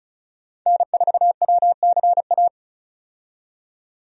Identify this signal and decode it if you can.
Morse code, heard as a keyed tone, about 36 wpm, 700 Hz. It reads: N4WCA
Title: USA Call Signs - 5 Characters - 35wpm